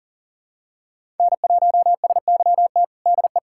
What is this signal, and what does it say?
Morse code, heard as a keyed tone, about 40 wpm, 700 Hz. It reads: N1SYT DE
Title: USA Callsign DE Callsign 40wpm